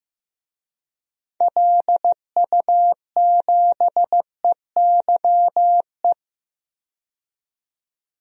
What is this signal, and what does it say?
Morse code, heard as a keyed tone, about 15 wpm, 700 Hz. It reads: LU7EYE
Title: DXCC-100 Call Signs - 6 Characters - 15wpm